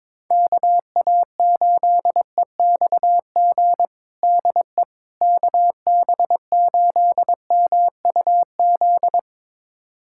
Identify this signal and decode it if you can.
Morse code, heard as a keyed tone, about 22 wpm, 700 Hz. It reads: KA8EXG DE KB8MUZ
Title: USA Callsign DE Callsign 22wpm